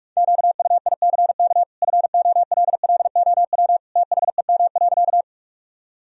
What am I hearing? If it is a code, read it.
Morse code, heard as a keyed tone, about 47 wpm, 700 Hz. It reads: QUICK FOLLOW THEM.
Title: Sentences from Top 1000 Words - Review Part 1 - 45wpm